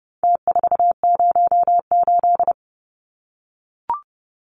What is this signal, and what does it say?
Morse code, heard as a keyed tone, about 30 wpm, 700 Hz. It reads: T408
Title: Numbers - Contest Style - 4-Digit 30wpm